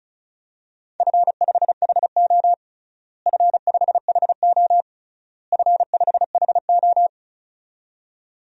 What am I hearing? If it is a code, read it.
Morse code, heard as a keyed tone, about 36 wpm, 690 Hz. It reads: F5HO F5HO F5HO
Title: DXCC-100 Call Signs - 4 Characters - Repeated 3x - 35wpm